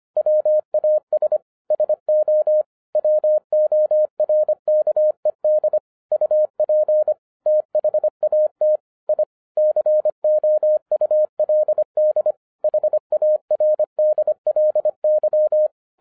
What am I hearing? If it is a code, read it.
Morse code, heard as a keyed tone, about 24 wpm, 610 Hz. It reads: WAS SO WORKED UP THAT I COULD HARDLY
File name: FIRST FISTS CONTACT-25wpm